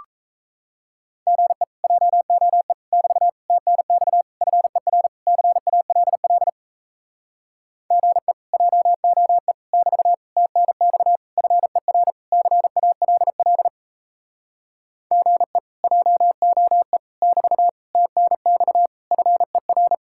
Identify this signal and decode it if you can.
Morse code, heard as a keyed tone, about 38 wpm, 700 Hz. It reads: GE JOE <BT> TNX FER CALL GE JOE <BT> TNX FER CALL GE JOE <BT> TNX FER